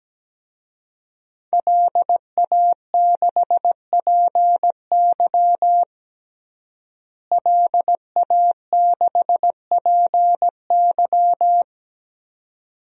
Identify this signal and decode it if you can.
Morse code, heard as a keyed tone, about 17 wpm, 700 Hz. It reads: LA6PY LA6PY
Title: DXCC-100 Call Signs - 5 Characters - Repeated 2x - 17wpm